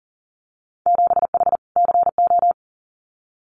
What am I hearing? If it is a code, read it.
Morse code, heard as a keyed tone, about 40 wpm, 700 Hz. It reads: 7H CO
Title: Winter Field Day Exchanges Encore 40wpm